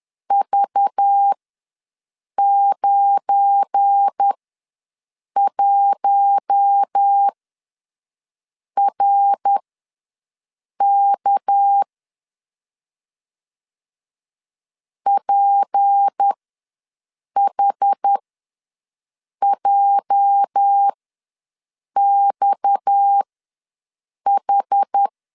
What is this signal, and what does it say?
Morse code, heard as a keyed tone, about 10 wpm, spaced as for 6 wpm, 800 Hz. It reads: V91RK PHJXH
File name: LEZIONI / 8 YJ45 / 8 YJ45 ws 10wpm 3843